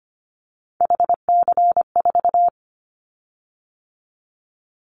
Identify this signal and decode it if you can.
Morse code, heard as a keyed tone, about 25 wpm, 700 Hz. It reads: HC4
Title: Callsign Prefixes for Top 100 DXCC - 25wpm